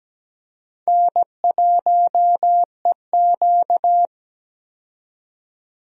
Morse code, heard as a keyed tone, about 17 wpm, 700 Hz. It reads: N1EQ
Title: USA Call Signs - 4 Characters - 17wpm